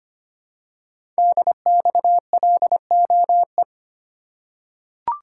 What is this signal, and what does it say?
Morse code, heard as a keyed tone, about 25 wpm, 700 Hz. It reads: DXLOE
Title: Code Groups - 5 Letters - 25wpm